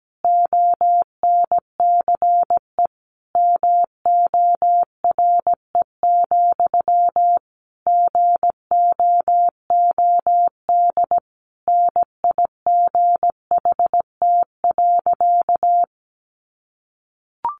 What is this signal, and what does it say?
Morse code, heard as a keyed tone, about 17 wpm, 700 Hz. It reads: ONCE MORE, GOOD NIGHT.
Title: Sentences from Top 300 Words 17wpm